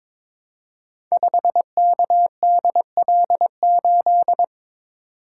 Morse code, heard as a keyed tone, about 22 wpm, 700 Hz. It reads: 5KDL8
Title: Code Groups - 5 Letter:Numbers - 22wpm